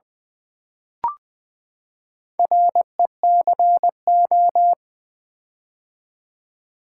Morse code, heard as a keyed tone, about 20 wpm, 700 Hz. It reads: RECO
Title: Build A Word - 4 Letters At A Time - 11-Letter Words - 20wpm